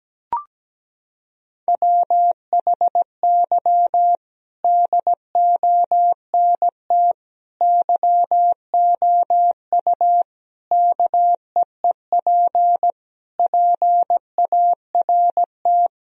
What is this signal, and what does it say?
Morse code, heard as a keyed tone, about 17 wpm, 700 Hz. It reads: WHY DONT YOU KEEP PART
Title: Sentences from Top 1000 Words - Review Part 4 - 17wpm